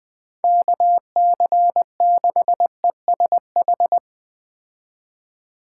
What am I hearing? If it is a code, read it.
Morse code, heard as a keyed tone, about 20 wpm, 700 Hz. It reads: KC6ESH
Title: Speed Racing - USA Call Signs 17wpm